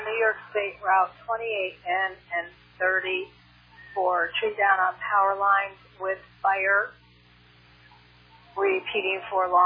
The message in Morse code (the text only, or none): none